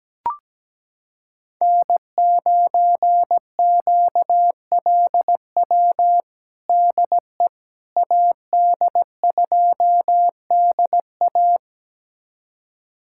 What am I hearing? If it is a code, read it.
Morse code, heard as a keyed tone, about 17 wpm, 700 Hz. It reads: N9QLW DE AD2DA
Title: USA Callsign DE Callsign - 5 Characters - 17wpm